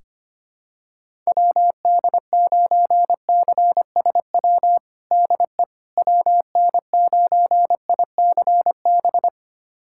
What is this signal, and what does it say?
Morse code, heard as a keyed tone, about 25 wpm, 700 Hz. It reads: WD9CSW DE WN9ICB
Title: USA Callsign DE Callsign 25wpm